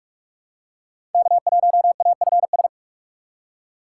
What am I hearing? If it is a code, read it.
Morse code, heard as a keyed tone, about 44 wpm, 700 Hz. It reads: K1AFS
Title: USA Call Signs - 5 Characters - 4x Character Spacing - 45wpm